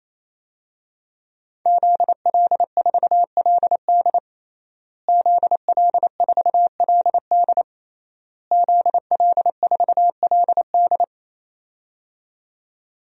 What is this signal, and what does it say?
Morse code, heard as a keyed tone, about 28 wpm, 690 Hz. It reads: ZL4LD ZL4LD ZL4LD
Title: DXCC-100 Call Signs - 5 Characters - Repeated 3x - 28wpm